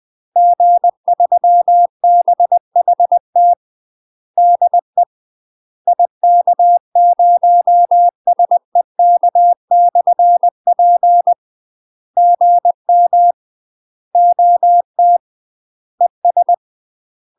Morse code, heard as a keyed tone, about 20 wpm, 700 Hz. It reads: G3BHT DE IK0SEK/P GM OT ES
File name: UK MASTER QSOs TRAINING / 20WPM / UK MASTER QSOs 0031